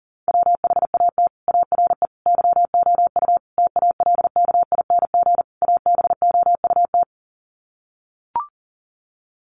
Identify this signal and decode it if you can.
Morse code, heard as a keyed tone, about 40 wpm, 700 Hz. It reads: WHAT ARE YOU TALKING ABOUT
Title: Sets of 5 Words 40wpm